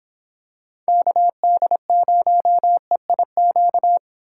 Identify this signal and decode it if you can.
Morse code, heard as a keyed tone, about 25 wpm, 700 Hz. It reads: KD0EIQ